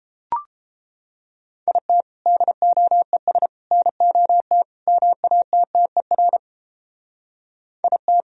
Morse code, heard as a keyed tone, about 30 wpm, 700 Hz. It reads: IT DOES NOT MATTER IT